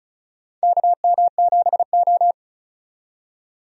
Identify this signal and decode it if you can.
Morse code, heard as a keyed tone, about 34 wpm, 700 Hz. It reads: KM7O
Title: USA Call Signs - 4 Characters - 35wpm